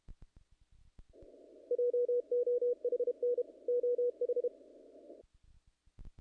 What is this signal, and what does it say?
Morse code, heard as a keyed tone, about 28 wpm, 470 Hz. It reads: JOHN OH
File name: contests / naqp-cw-k5tr-op-k5pi- -1800